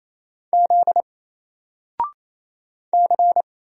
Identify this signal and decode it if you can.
Morse code, heard as a keyed tone, about 28 wpm, 700 Hz. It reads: Z C
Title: Single Letter-Number Alt 28wpm